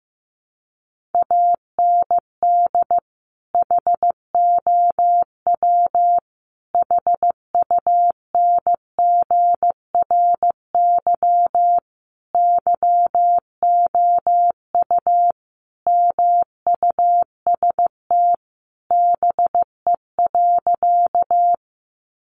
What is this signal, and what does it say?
Morse code, heard as a keyed tone, about 15 wpm, 700 Hz. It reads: AND HOW HUNGRY YOU MUST BE.
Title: Sentences from Top 1400 Words 15wpm